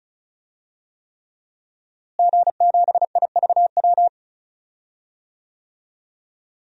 Morse code, heard as a keyed tone, about 36 wpm, 700 Hz. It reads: G7IVW
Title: DXCC-100 Call Signs - 5 Characters - Repeated 3x - 35wpm